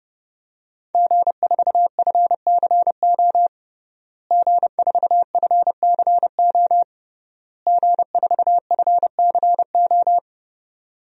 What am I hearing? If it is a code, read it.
Morse code, heard as a keyed tone, about 30 wpm, 700 Hz. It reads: G4FCO G4FCO G4FCO
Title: DXCC-100 Call Signs - 5 Characters - Repeated 3x - 30wpm